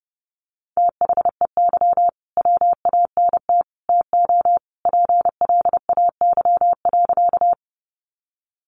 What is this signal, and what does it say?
Morse code, heard as a keyed tone, about 30 wpm, 700 Hz. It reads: THEY WANT TO PLAY.
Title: Easy Phrases 30wpm